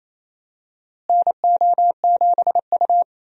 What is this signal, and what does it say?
Morse code, heard as a keyed tone, about 28 wpm, 700 Hz. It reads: NO7U
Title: USA Call Signs - 4 Characters - Repeated 2x - 28wpm